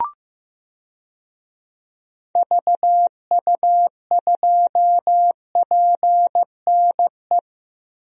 Morse code, heard as a keyed tone, about 15 wpm, 700 Hz. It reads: VU2PNE
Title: DXCC-100 Call Signs - 6 Characters - Repeated 2x - 15wpm